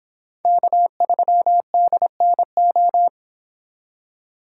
Morse code, heard as a keyed tone, about 22 wpm, 700 Hz. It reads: K3DNO
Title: Speed Racing - USA 1x3 Call Signs 22wpm